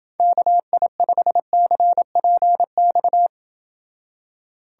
Morse code, heard as a keyed tone, about 28 wpm, 700 Hz. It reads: KI5CPX